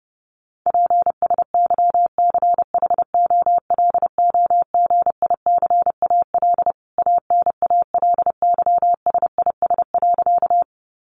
Morse code, heard as a keyed tone, about 30 wpm, 700 Hz. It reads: PSYCHOLOGICAL ANALYSIS.
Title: Sentences from Top 1400 Words 30wpm